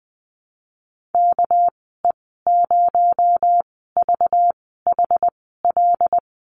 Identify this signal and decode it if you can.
Morse code, heard as a keyed tone, about 20 wpm, spaced as for 15 wpm, 700 Hz. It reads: KE0VHL
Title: 219 Lesson 35g - Number 0 - Call Signs 20wpm